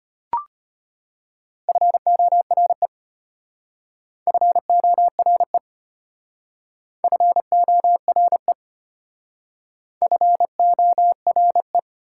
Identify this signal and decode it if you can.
Morse code, heard as a keyed tone, about 32 wpm, 700 Hz. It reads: FORE FORE FORE FORE